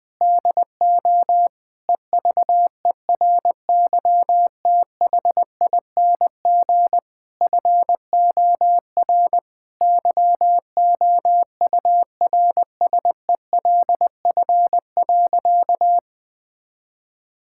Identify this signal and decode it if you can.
Morse code, heard as a keyed tone, about 21 wpm, 700 Hz. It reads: DO EVERYTHING FOR YOURSELF.
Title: Sentences from Top 1000 Words - Review Part 6 - 20wpm